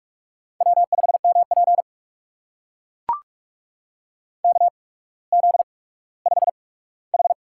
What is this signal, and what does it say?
Morse code, heard as a keyed tone, about 44 wpm, spaced as for 19 wpm, 700 Hz. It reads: W5MP KZ5H
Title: USA Call Signs - 4 Characters - 8x Character Spacing - 45wpm